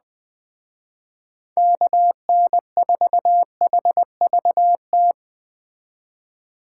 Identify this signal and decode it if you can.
Morse code, heard as a keyed tone, about 20 wpm, 700 Hz. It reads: KN4HVT
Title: Speed Racing - USA 2x3 Call Signs 17wpm